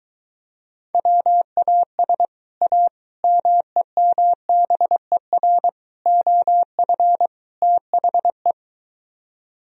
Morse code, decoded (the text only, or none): WAS A MEMBER OF THE